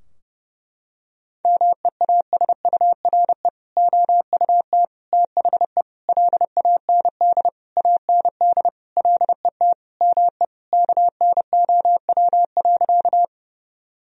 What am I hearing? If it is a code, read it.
Morse code, heard as a keyed tone, about 30 wpm, 700 Hz. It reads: MEASURE OUT THE LAND AND LET ME KNOW.
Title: Sentences from Top 1000 Words - Review Part 2 - 30wpm